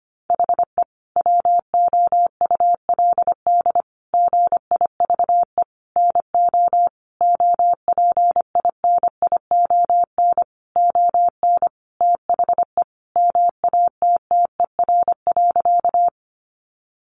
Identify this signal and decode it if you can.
Morse code, heard as a keyed tone, about 25 wpm, 700 Hz. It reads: HE WOULD GIVE NO OPINION ON THE MATTER.
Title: Sentences from Top 1200 Words 25wpm